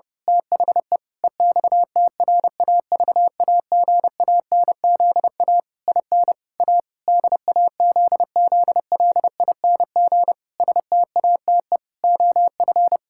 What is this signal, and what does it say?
Morse code, heard as a keyed tone, about 30 wpm, 700 Hz. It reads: THE EXTRAVAGANZA IN A DAZZLING STATE OF
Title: Pangram Challenge v2 - 30wpm